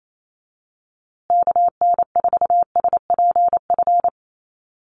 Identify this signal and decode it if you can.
Morse code, heard as a keyed tone, about 28 wpm, 700 Hz. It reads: KN4SPF
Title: USA Call Signs Repeated 3x 28wpm